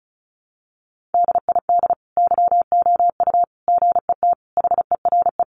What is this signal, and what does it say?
Morse code, heard as a keyed tone, about 34 wpm, 700 Hz. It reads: DID YOU GET HERE
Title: Sentences from Top 700 Words 35wpm